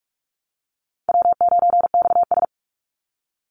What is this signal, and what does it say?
Morse code, heard as a keyed tone, about 44 wpm, 700 Hz. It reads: W9XS
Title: Call Signs - 45wpm